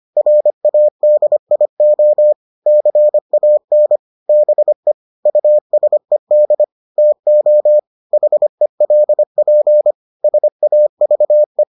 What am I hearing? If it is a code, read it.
Morse code, heard as a keyed tone, about 25 wpm, 600 Hz. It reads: RADIO CAN BE USED TO HELP SAVE